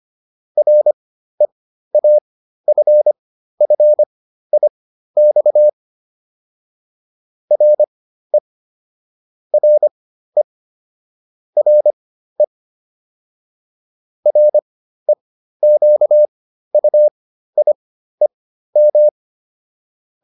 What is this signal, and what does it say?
Morse code, heard as a keyed tone, about 25 wpm, spaced as for 13 wpm, 600 Hz. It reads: REAFFIX RE RE RE REQUIEM